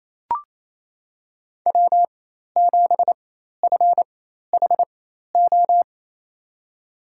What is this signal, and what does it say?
Morse code, heard as a keyed tone, about 28 wpm, spaced as for 13 wpm, 700 Hz. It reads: W7FHO
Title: USA Call Signs - 4x Character Spacing - 28wpm